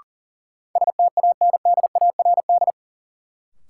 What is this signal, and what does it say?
Morse code, heard as a keyed tone, about 40 wpm, 700 Hz. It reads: STANDARD
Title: Sentences from Top 1000 Words - Review Part 6 - New Word Drill - 40wpm